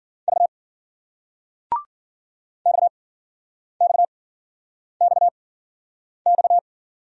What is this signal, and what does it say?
Morse code, heard as a keyed tone, about 53 wpm, spaced as for 9 wpm, 700 Hz. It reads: V XXXX